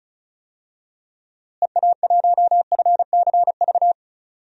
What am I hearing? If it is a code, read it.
Morse code, heard as a keyed tone, about 34 wpm, 700 Hz. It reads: EA1FCV
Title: DXCC-100 Call Signs - 6 Characters - Repeated 2x - 35wpm